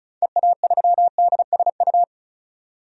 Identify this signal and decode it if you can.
Morse code, heard as a keyed tone, about 36 wpm, 700 Hz. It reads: EA3DSU
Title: DXCC-100 Call Signs - 6 Characters - 35wpm